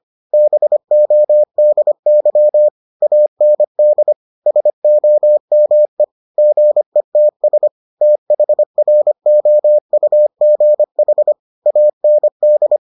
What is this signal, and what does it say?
Morse code, heard as a keyed tone, about 25 wpm, 600 Hz. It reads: BODY AND SOME GETS THROUGH AND